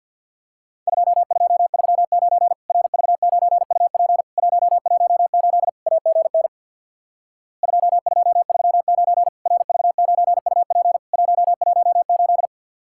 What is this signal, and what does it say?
Morse code, heard as a keyed tone, about 52 wpm, 700 Hz. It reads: 2239 RV9UP 118 AGN 2239 RV9UP 118
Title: Worked All Europe DX Contest - QTC Exchange with Repeats - 50wpm